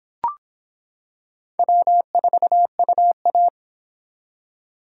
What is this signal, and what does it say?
Morse code, heard as a keyed tone, about 26 wpm, 700 Hz. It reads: W4UA